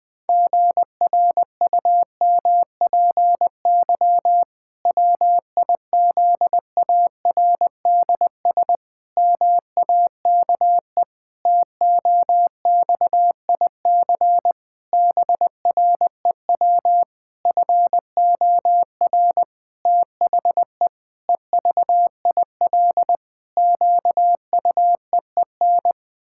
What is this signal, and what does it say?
Morse code, heard as a keyed tone, about 21 wpm, 700 Hz. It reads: GRUMPY WIZARDS MAKE TOXIC BREW FOR THE EVIL QUEEN
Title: Pangram Challenge v2 - 20wpm